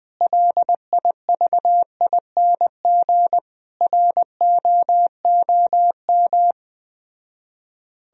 Morse code, heard as a keyed tone, about 21 wpm, 700 Hz. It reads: LIVING ROOM